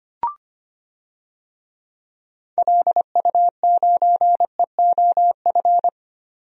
Morse code, heard as a keyed tone, about 25 wpm, 700 Hz. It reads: LU9EOF
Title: DXCC-100 Call Signs - 6 Characters - 25wpm